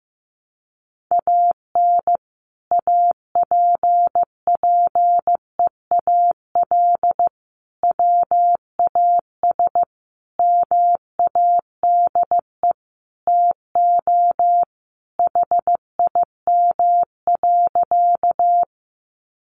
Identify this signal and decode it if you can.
Morse code, heard as a keyed tone, about 15 wpm, 700 Hz. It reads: AN APPEAL WAS MADE TO HIM.
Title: Sentences from Top 1300 Words 15wpm